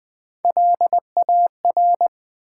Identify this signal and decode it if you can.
Morse code, heard as a keyed tone, about 20 wpm, 700 Hz. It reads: LAR
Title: Build A Word - 1 Letter At A Time - 5-Letter Words - 20wpm